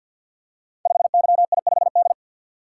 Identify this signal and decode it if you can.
Morse code, heard as a keyed tone, about 51 wpm, 700 Hz. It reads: 5YI5D
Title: Code Groups - 5 Letter:Numbers - 50wpm